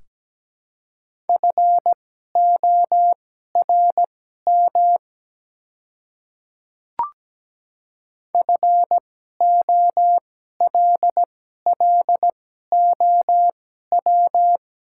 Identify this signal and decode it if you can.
Morse code, heard as a keyed tone, about 17 wpm, spaced as for 12 wpm, 700 Hz. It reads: FORM FOLLOW
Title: 2x Character Spacing - Top 200 Words - 17wpm